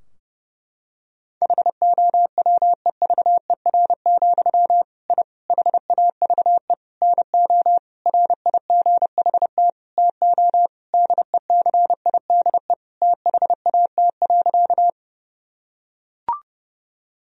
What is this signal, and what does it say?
Morse code, heard as a keyed tone, about 30 wpm, 700 Hz. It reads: HOWEVER, I HAVE NO RIGHT TO DECIDE THAT.
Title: Sentences from Top 1000 Words - Review Part 3 - 30wpm